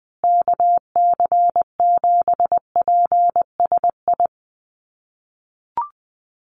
Morse code, heard as a keyed tone, about 20 wpm, 700 Hz. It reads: KC7PSI
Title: USA Call Signs Repeated 3x 20wpm